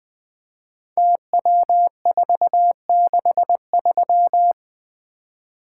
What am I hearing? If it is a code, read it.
Morse code, heard as a keyed tone, about 20 wpm, 700 Hz. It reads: TW463
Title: Code Groups - 5 Letter:Numbers - 20wpm